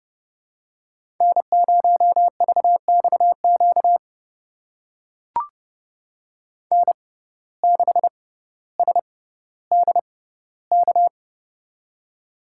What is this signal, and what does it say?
Morse code, heard as a keyed tone, about 30 wpm, spaced as for 15 wpm, 700 Hz. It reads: N0VXQ N6SDK
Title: USA Call Signs - 6x Character Spacing - 30wpm